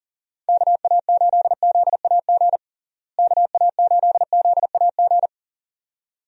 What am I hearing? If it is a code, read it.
Morse code, heard as a keyed tone, about 40 wpm, 700 Hz. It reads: KA8ZAG KA8ZAG
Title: USA Call Signs - 6 Characters - Repeated 2x - 40wpm